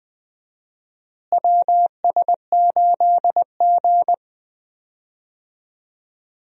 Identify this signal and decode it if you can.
Morse code, heard as a keyed tone, about 21 wpm, 700 Hz. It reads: WS8G
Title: USA Call Signs - 4 Characters - Repeated 3x - 20wpm